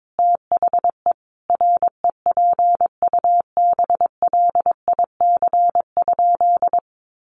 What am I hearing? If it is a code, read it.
Morse code, heard as a keyed tone, about 22 wpm, 700 Hz. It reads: THE REPUBLIC?
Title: Sentences from Top 1500 Words 22wpm